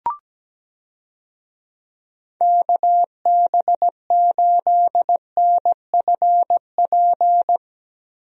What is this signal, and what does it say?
Morse code, heard as a keyed tone, about 17 wpm, 700 Hz. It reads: KB8NFP
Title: USA Call Signs - 6 Characters - Repeated 2x - 17wpm